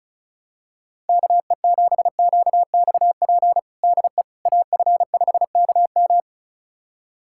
Morse code, heard as a keyed tone, about 36 wpm, 700 Hz. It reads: KE7QXP DE AF5KM
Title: USA Callsign DE Callsign 35wpm